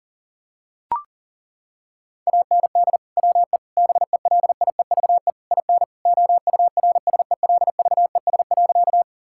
Morse code, heard as a keyed tone, about 37 wpm, 700 Hz. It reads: AND WE BELIEVE IN OURSELVES.
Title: Sentences from Top 1600 Words 40wpm